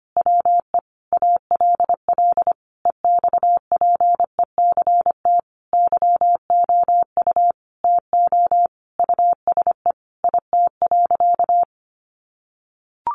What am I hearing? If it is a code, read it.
Morse code, heard as a keyed tone, about 25 wpm, 700 Hz. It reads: WE ALL EXPECT YOU TO USE IT.
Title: Sentences from Top 1100 Words 25wpm